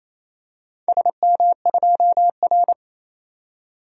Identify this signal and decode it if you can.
Morse code, heard as a keyed tone, about 28 wpm, 700 Hz. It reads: SM2R
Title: DXCC-100 Call Signs - 4 Characters - 28wpm